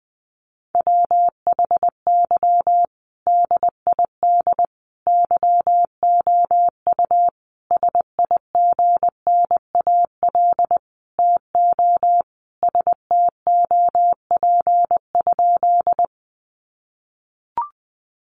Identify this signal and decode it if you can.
Morse code, heard as a keyed tone, about 19 wpm, 700 Hz. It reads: WHY DID YOU SIGNAL TO STOP?
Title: Sentences from Top 1400 Words 20wpm